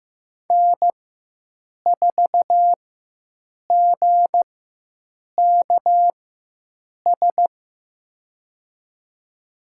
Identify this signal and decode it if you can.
Morse code, heard as a keyed tone, about 15 wpm, spaced as for 7 wpm, 700 Hz. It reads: N4GKS
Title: USA Call Signs - 5 Characters - 4x Character Spacing - 15wpm